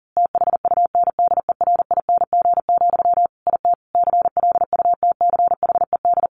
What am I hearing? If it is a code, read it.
Morse code, heard as a keyed tone, about 37 wpm, 700 Hz. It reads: THUNDERING, IT CLUTCHED
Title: A Sound Of Thunder 40wpm